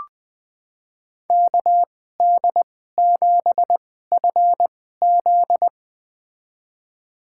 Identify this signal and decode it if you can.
Morse code, heard as a keyed tone, about 20 wpm, spaced as for 14 wpm, 700 Hz. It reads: KD7FZ
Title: USA Call Signs - 5 Characters - 2x Character Spacing - 20wpm